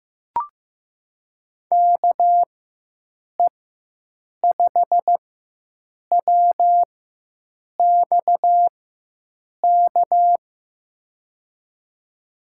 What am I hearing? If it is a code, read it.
Morse code, heard as a keyed tone, about 15 wpm, spaced as for 7 wpm, 700 Hz. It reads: KE5WXK
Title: USA Call Signs - 6 Characters - 4x Character Spacing - 15wpm